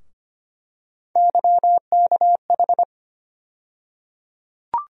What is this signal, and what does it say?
Morse code, heard as a keyed tone, about 25 wpm, 700 Hz. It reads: YKH
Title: USA Call Sign Postfixes - 3 Letters 25wpm